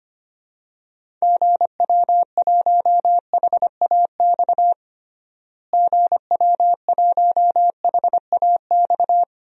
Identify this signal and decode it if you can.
Morse code, heard as a keyed tone, about 25 wpm, 700 Hz. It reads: GW1HAX GW1HAX
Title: DXCC-100 Call Signs - 6 Characters - Repeated 3x - 25wpm